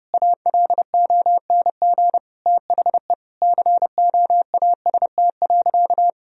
Morse code, heard as a keyed tone, about 30 wpm, 700 Hz. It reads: ALONG THE COAST.
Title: Sentences from Top 1000 Words - Review Part 4 - 30wpm